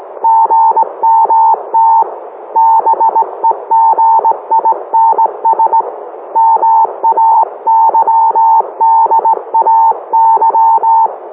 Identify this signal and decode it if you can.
Morse code, heard as a keyed tone, about 18 wpm, 900 Hz. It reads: GMT BEGINS MAYDAY